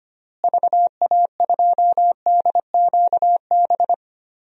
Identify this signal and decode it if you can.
Morse code, heard as a keyed tone, about 25 wpm, 700 Hz. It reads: VA2DQB
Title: DXCC-100 Call Signs - 6 Characters - Repeated 3x - 25wpm